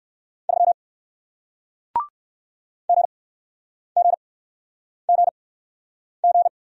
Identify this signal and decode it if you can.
Morse code, heard as a keyed tone, about 60 wpm, spaced as for 9 wpm, 700 Hz. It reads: 3 GGGG